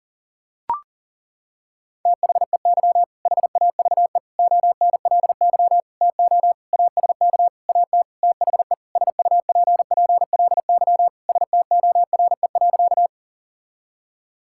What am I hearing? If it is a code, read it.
Morse code, heard as a keyed tone, about 40 wpm, 700 Hz. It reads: THEY HAVE ONLY TO ASK AT THE SUPPLY STORE.
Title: Sentences from Top 1000 Words - Review Part 5 - 40wpm